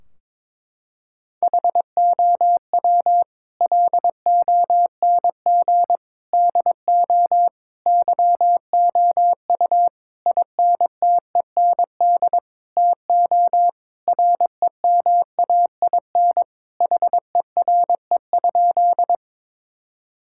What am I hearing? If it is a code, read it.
Morse code, heard as a keyed tone, about 22 wpm, 700 Hz. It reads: HOW LONG DO YOU INTEND TO REMAIN HERE?
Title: Sentences from Top 1600 Words 22wpm